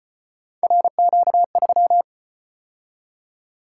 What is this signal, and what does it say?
Morse code, heard as a keyed tone, about 33 wpm, 700 Hz. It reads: RQ3